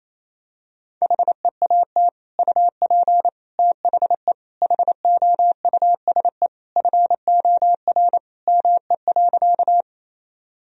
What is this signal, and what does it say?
Morse code, heard as a keyed tone, about 28 wpm, 700 Hz. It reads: HEAT UP THE HOUSE FOR ME.
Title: Sentences from Top 1000 Words - Review Part 3 - 28wpm